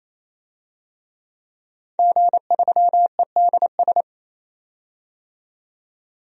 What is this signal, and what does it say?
Morse code, heard as a keyed tone, about 28 wpm, 700 Hz. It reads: G3EDS
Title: DXCC-100 Call Signs - 5 Characters - 28wpm